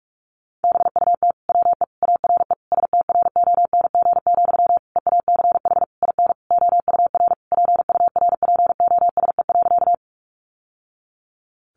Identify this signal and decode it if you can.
Morse code, heard as a keyed tone, about 44 wpm, 700 Hz. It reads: BUT WE ARE STRONG, EACH IN OUR PURPOSE.
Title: Sentences from Top 800 Words 45wpm